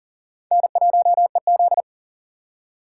Morse code, heard as a keyed tone, about 39 wpm, 700 Hz. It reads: N1EZ
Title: USA Call Signs - 4x Character Spacing - 40wpm